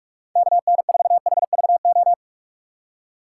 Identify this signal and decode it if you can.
Morse code, heard as a keyed tone, about 41 wpm, 700 Hz. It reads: KN4HVO
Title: USA Call Signs - 6 Characters - 6x Character Spacing - 45wpm